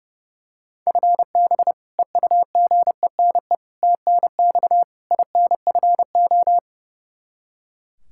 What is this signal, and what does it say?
Morse code, heard as a keyed tone, about 30 wpm, 700 Hz. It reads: FB EUGENE TNX INFO
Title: Practice QSO - Exchange 3 - More Info - 30wpm